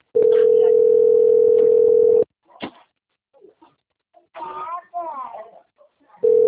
Ringback tone (North America), 440 + 480 Hz.
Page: 191.252.59.82 Recife - PE